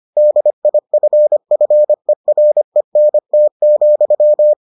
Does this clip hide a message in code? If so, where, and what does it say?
Yes: Morse code from 0.2 to 4.5 s, heard as a keyed tone, about 25 wpm, 600 Hz. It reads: DIFFERENT,